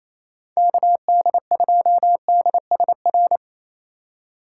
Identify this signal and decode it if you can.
Morse code, heard as a keyed tone, about 28 wpm, 700 Hz. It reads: KD2DSR
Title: USA Call Signs - 6 Characters - 8x Character Spacing - 28wpm